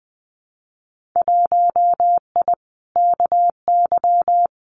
Morse code, heard as a keyed tone, about 20 wpm, 700 Hz. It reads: 1I KY
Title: Winter Field Day Exchanges 20wpm